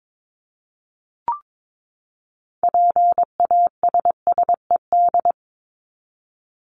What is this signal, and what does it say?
Morse code, heard as a keyed tone, about 22 wpm, 700 Hz. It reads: PASSED
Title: ED Suffix - Top 100 Words with Base Words 22wpm